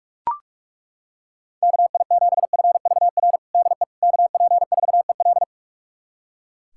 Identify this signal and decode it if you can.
Morse code, heard as a keyed tone, about 41 wpm, 700 Hz. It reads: KI7FVR DE KP4EL
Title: USA Callsign DE Callsign 45wpm